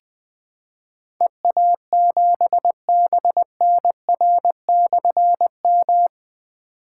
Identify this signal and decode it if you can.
Morse code, heard as a keyed tone, about 20 wpm, 700 Hz. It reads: EA7BNR/M
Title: Difficult Call Signs 20wpm